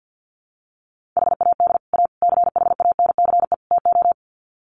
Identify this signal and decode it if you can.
Morse code, heard as a keyed tone, about 50 wpm, 700 Hz. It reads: HAD A CHANCE TO
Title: Sets of 4 Words 50wpm